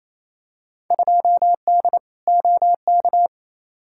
Morse code, heard as a keyed tone, about 28 wpm, 700 Hz. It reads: 2D OK
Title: Field Day Exchanges Encore - 28wpm